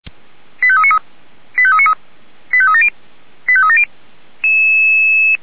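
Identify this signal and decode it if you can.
Five-tone selective call (ZVEI), heard as quick runs of five tones, tones of about 70 ms. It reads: 85393 85393 85370 85370
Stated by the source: Five-tone ZVEI selective call received by SDR.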